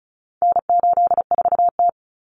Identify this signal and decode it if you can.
Morse code, heard as a keyed tone, about 34 wpm, 700 Hz. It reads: N84T
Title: Common Cut Numbers - 4-Digit 35wpm